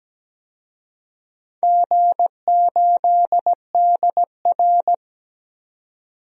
Morse code, heard as a keyed tone, about 17 wpm, 700 Hz. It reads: G8DR
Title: DXCC-100 Call Signs - 4 Characters - 17wpm